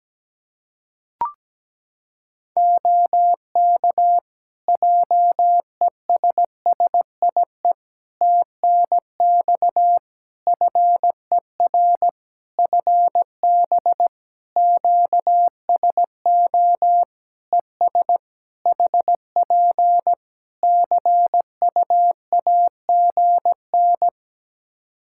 Morse code, heard as a keyed tone, about 17 wpm, 700 Hz. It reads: OK JESSIE TNX FER FB QSO ES HP CUAGN
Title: Practice QSO - Exchange 4 - Ending - 17wpm